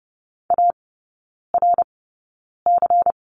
Morse code, heard as a keyed tone, about 30 wpm, spaced as for 9 wpm, 690 Hz. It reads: ARC
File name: Lesson 17e - Letter u - Three Characters 30wpm